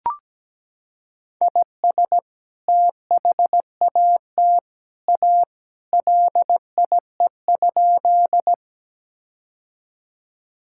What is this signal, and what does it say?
Morse code, heard as a keyed tone, about 17 wpm, 700 Hz. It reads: IS THAT A LIE?
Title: Sentences from Top 1600 Words 17wpm